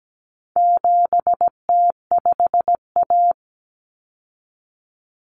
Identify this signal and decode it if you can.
Morse code, heard as a keyed tone, about 17 wpm, 700 Hz. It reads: 7T5A
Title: Common Cut Numbers - 4-Digit 17wpm